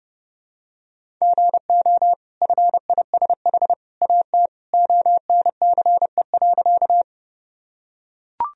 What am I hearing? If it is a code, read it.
Morse code, heard as a keyed tone, about 30 wpm, 700 Hz. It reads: GO FISH AT ONCE.
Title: Sentences from Top 1000 Words - Review Part 1 - 30wpm